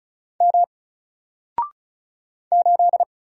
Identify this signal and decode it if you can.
Morse code, heard as a keyed tone, about 33 wpm, 700 Hz. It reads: M 8
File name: Single-Letter-Number-35wpm